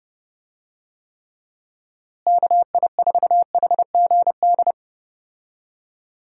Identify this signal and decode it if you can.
Morse code, heard as a keyed tone, about 30 wpm, 700 Hz. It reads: KI4HGD
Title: USA Call Signs - 6 Characters - Repeated 2x - 30wpm